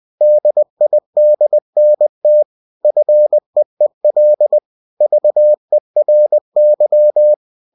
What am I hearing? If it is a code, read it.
Morse code, heard as a keyed tone, about 20 wpm, 600 Hz. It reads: DIDNT FEEL VERY
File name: cw-books / winnie-the-pooh / winnie-20wpm-page-0040